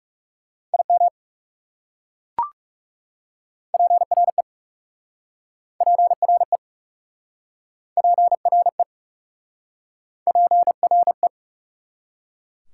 Morse code, heard as a keyed tone, about 38 wpm, 690 Hz. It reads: IM PRE PRE PRE PRE